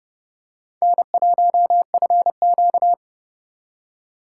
Morse code, heard as a keyed tone, about 28 wpm, 700 Hz. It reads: N1FQ